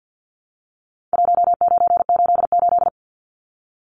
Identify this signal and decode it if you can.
Morse code, heard as a keyed tone, about 50 wpm, 700 Hz. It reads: 1988
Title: Numbers - Contest Style - 4-Digit 50wpm